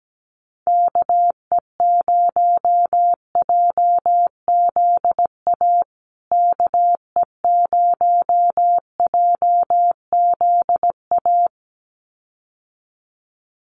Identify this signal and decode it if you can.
Morse code, heard as a keyed tone, about 17 wpm, 700 Hz. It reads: KE0JZA KE0JZA
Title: USA Call Signs Repeated 2x 17wpm